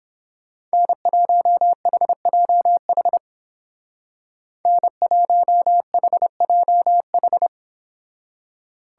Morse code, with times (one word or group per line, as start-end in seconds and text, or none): 0.7-3.2 N1HJH
4.7-7.5 N1HJH